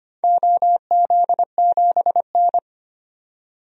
Morse code, heard as a keyed tone, about 25 wpm, 700 Hz. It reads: OZ7N
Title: DXCC-100 Call Signs - 4 Characters - 25wpm